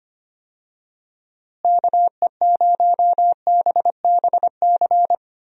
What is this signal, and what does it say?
Morse code, heard as a keyed tone, about 25 wpm, 700 Hz. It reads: KE0BBC
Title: USA Call Signs - 6 Characters - Repeated 2x - 25wpm